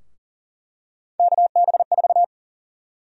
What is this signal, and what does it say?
Morse code, heard as a keyed tone, about 40 wpm, 700 Hz. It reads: KB4
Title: USA Call Sign Prefixes - 40wpm